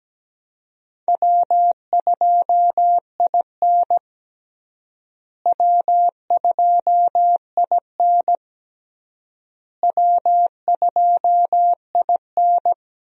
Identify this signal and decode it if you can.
Morse code, heard as a keyed tone, about 17 wpm, 700 Hz. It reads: W2IN W2IN W2IN
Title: USA Call Signs - 4 Characters - Repeated 3x - 17wpm